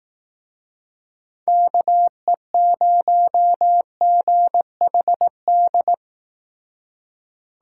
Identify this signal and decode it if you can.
Morse code, heard as a keyed tone, about 19 wpm, 700 Hz. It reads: KE0GHD
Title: Speed Racing - USA 2x3 Call Signs 15wpm